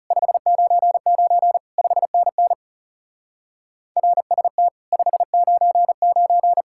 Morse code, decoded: 599 5NN RST 599

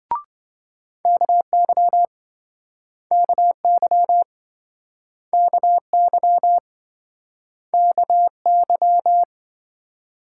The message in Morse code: KY KY KY KY

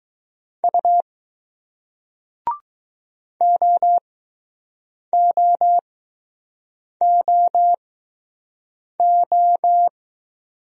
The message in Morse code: U OOOO

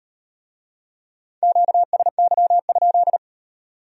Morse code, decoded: QSY?

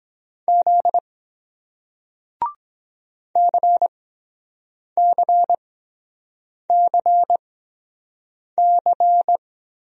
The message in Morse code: Z CCCC

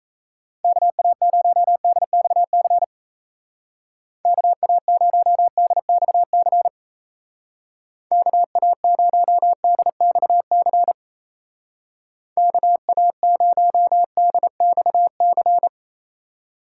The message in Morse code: KA0DXC KA0DXC KA0DXC KA0DXC

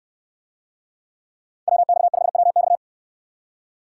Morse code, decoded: WX5WX

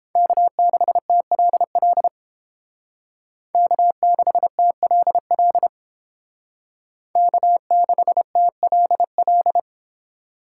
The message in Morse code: K6TLL K6TLL K6TLL

